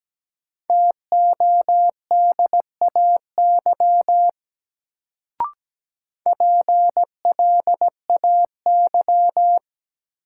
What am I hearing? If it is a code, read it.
Morse code, heard as a keyed tone, about 17 wpm, 700 Hz. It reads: TODAY PLAY
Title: Sets of 5 Words Encore - New Word Drill - 17wpm